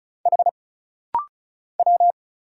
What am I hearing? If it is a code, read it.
Morse code, heard as a keyed tone, about 34 wpm, 700 Hz. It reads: H W
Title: Single Letter-Number Alt - Rapid-Fire 35wpm